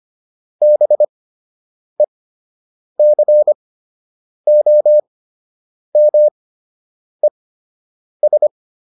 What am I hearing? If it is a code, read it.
Morse code, heard as a keyed tone, about 25 wpm, spaced as for 8 wpm, 600 Hz. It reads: BECOMES